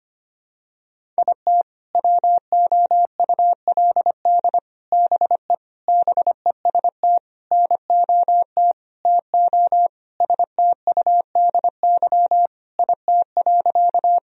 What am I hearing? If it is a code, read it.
Morse code, heard as a keyed tone, about 25 wpm, 700 Hz. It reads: IT WOULD BE BEST NOT TO STUDY IT.
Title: Sentences from Top 1000 Words - Review Part 1 - 25wpm